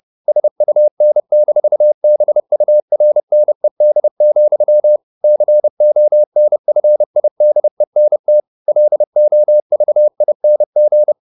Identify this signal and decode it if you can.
Morse code, heard as a keyed tone, about 30 wpm, 600 Hz. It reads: SUN-BURNED, CONFIDENT LOVING
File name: the old man and the sea / the old man and the sea 1 30wpm